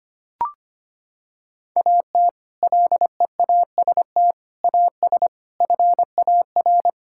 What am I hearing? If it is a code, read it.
Morse code, heard as a keyed tone, about 25 wpm, 700 Hz. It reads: AT LEAST AS FAR
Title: Sentences from Top 1000 Words - Review Part 6 - 25wpm